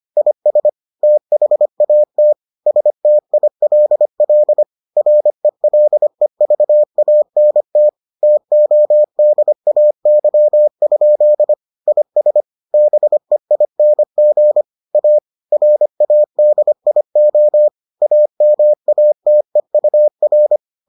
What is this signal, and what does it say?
Morse code, heard as a keyed tone, about 25 wpm, 600 Hz. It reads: IS THAT STILL RELEVANT TODAY? IS BEING A RADIO AMATEUR